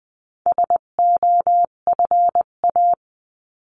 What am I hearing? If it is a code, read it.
Morse code, heard as a keyed tone, about 20 wpm, 700 Hz. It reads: SOFA
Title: 258 Lesson 42c - Words 16-20wpm